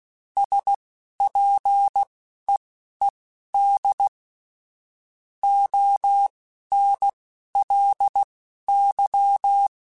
Morse code, heard as a keyed tone, about 16 wpm, spaced as for 12 wpm, 800 Hz. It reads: SPEED ONLY
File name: Weather Sets / Weather Sets-mp3s / Weather 5 1 @10